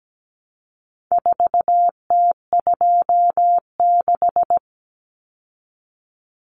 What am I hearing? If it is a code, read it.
Morse code, heard as a keyed tone, about 17 wpm, 700 Hz. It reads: 4T26
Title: Common Cut Numbers - 4-Digit 17wpm